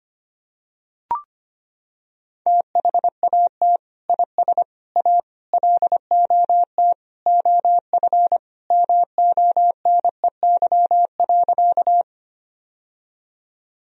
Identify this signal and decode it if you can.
Morse code, heard as a keyed tone, about 25 wpm, 700 Hz. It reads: THAT IS A LOT OF MONEY.
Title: Sentences from Top 1000 Words - Review Part 3 - 25wpm